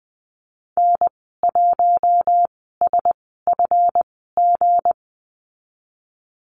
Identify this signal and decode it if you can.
Morse code, heard as a keyed tone, about 20 wpm, spaced as for 15 wpm, 700 Hz. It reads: N1SFG
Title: 179 Lesson 29g - Slash - Call Signs 20wpm